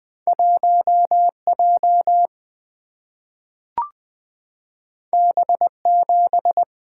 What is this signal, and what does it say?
Morse code, heard as a keyed tone, about 20 wpm, 700 Hz. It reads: 1J B7
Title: Single-Letter Single-Number 20wpm